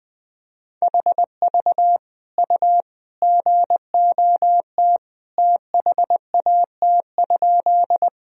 Morse code, heard as a keyed tone, about 20 wpm, 700 Hz. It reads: HV U GOT THAT?
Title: Sentences from Top 300 Words and Abbreviations 20wpm